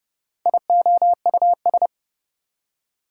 Morse code, heard as a keyed tone, about 37 wpm, 700 Hz. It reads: IOUS
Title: Speed-Racing Prefixes and Suffixes 30wpm